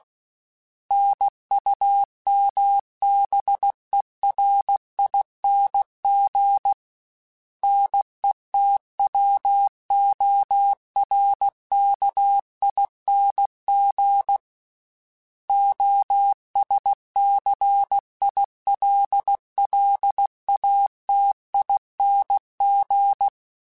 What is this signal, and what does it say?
Morse code, heard as a keyed tone, about 16 wpm, 800 Hz. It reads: NUMBERING NETWORKING OSCILLATING
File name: Special INGs / Special INGs-mp3s / Sngl ing @15